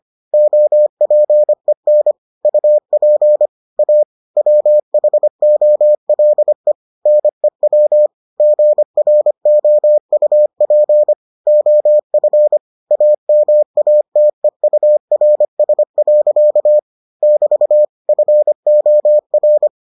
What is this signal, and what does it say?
Morse code, heard as a keyed tone, about 25 wpm, 600 Hz. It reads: OPEN UP A WHOLE NEW GROUP OF AMATEURS. <BT> FOR